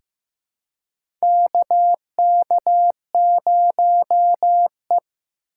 Morse code, heard as a keyed tone, about 15 wpm, 700 Hz. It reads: KK0E
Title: USA Call Signs - 4 Characters - 2x Character Spacing - 15wpm